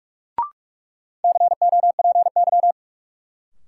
Morse code, heard as a keyed tone, about 45 wpm, 700 Hz. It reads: COPY
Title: Sentences from Top 1400 Words - New Word Drill Alt - 45wpm